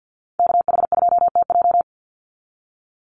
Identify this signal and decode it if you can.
Morse code, heard as a keyed tone, about 50 wpm, 700 Hz. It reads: KH2TJ
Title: NA SOTA Chaser Callsigns 50wpm